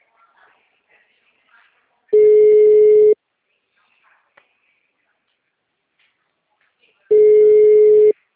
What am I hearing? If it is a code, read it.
Ringback tone (Europe, 425 Hz).
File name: out- -2010- - - .90